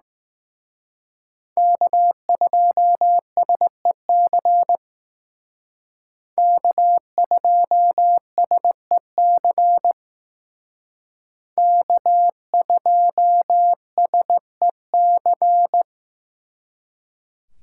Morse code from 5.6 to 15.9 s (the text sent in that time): K2SEC K2SEC